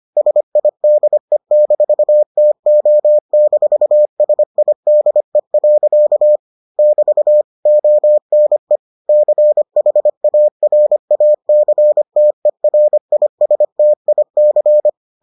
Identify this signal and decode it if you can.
Morse code, heard as a keyed tone, about 25 wpm, 600 Hz. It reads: SIDE-TO-SIDE. <BT> ONE CHARACTERISTIC